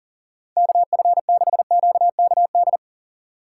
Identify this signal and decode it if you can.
Morse code, heard as a keyed tone, about 40 wpm, 700 Hz. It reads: KF6QKD
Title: USA Call Signs - 6 Characters - 4x Character Spacing - 40wpm